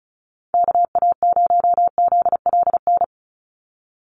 Morse code, heard as a keyed tone, about 34 wpm, 700 Hz. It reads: KA0ZLN
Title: USA Call Signs 35wpm